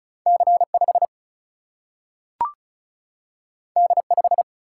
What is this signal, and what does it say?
Morse code, heard as a keyed tone, about 36 wpm, 700 Hz. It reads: C5 D5
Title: Single-Letter Single-Number 35wpm